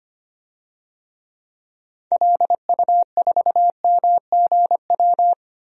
Morse code, heard as a keyed tone, about 25 wpm, 700 Hz. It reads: LU4MGW
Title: DXCC-100 Call Signs - 6 Characters - 25wpm